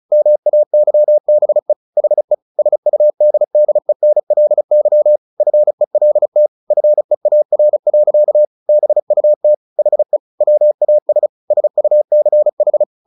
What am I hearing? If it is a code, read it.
Morse code, heard as a keyed tone, about 34 wpm, 600 Hz. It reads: MAYBE HE SUDDENLY FELT FEAR. BUT HE WAS SUCH
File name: the old man and the sea 2 35wpm